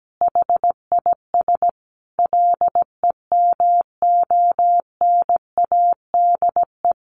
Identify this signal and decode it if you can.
Morse code, heard as a keyed tone, about 17 wpm, 700 Hz. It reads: HIS LEMONADE
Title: Additional Easy Phrases 17wpm